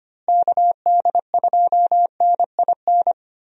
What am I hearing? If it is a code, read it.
Morse code, heard as a keyed tone, about 25 wpm, 700 Hz. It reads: KD2NIN
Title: USA Call Signs - 6 Characters - Repeated 3x - 25wpm